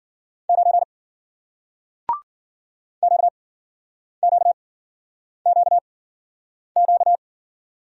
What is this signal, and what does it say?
Morse code, heard as a keyed tone, about 54 wpm, spaced as for 9 wpm, 700 Hz. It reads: 9 QQQQ